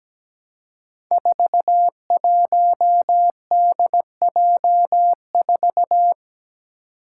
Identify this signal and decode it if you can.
Morse code, heard as a keyed tone, about 17 wpm, 700 Hz. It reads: 41DJ4
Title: Five Challenging Letter-Number Combinations 17wpm